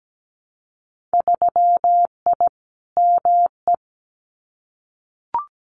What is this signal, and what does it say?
Morse code, heard as a keyed tone, about 17 wpm, 700 Hz. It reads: 3I ME
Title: Winter Field Day Exchanges 17wpm